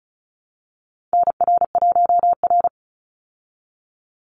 Morse code, heard as a keyed tone, about 34 wpm, 700 Hz. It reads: NR1R
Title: Call Signs Repeated 2x - 35wpm